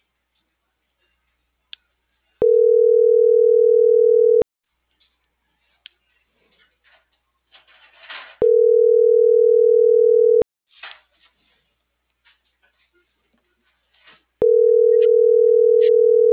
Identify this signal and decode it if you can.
Ringback tone (North America), 440 + 480 Hz.